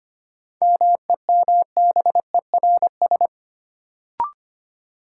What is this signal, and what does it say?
Morse code, heard as a keyed tone, about 25 wpm, 700 Hz. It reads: MEMBERS
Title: Sets of 3 Words - New Word Drill - 25wpm